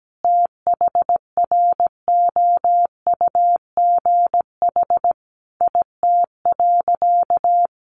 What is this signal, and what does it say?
Morse code, heard as a keyed tone, about 17 wpm, 700 Hz. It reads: THROUGH IT.
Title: Sentences from Top 200 Words 17wpm